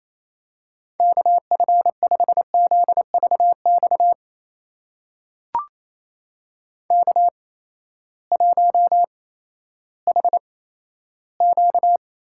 Morse code, heard as a keyed tone, about 28 wpm, spaced as for 14 wpm, 700 Hz. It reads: KF5ZVX K1HQ
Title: USA Call Signs - 8x Character Spacing - 28wpm